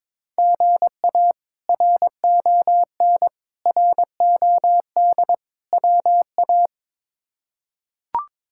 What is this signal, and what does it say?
Morse code, heard as a keyed tone, about 22 wpm, 700 Hz. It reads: GA RON ROD WA
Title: K1USN SST QSO Practice - 22wpm